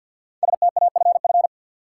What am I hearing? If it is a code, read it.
Morse code, heard as a keyed tone, about 53 wpm, 700 Hz. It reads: STAFF
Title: Sentences from Top 1200 Words - New Word Drill - 50wpm